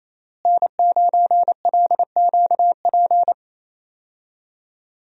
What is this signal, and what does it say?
Morse code, heard as a keyed tone, about 27 wpm, 700 Hz. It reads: N9LQP
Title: USA Call Signs - 5 Characters - Repeated 3x - 28wpm